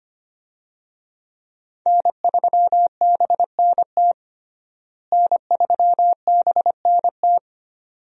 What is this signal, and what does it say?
Morse code, heard as a keyed tone, about 25 wpm, 700 Hz. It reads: N3BNT N3BNT
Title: USA Call Signs - 5 Characters - Repeated 2x - 25wpm